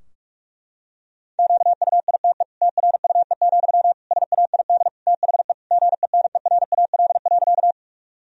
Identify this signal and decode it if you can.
Morse code, heard as a keyed tone, about 46 wpm, 700 Hz. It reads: QUITE TRUE, SAID THE GENERAL.
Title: Sentences from Top 1000 Words - Review Part 3 - 45wpm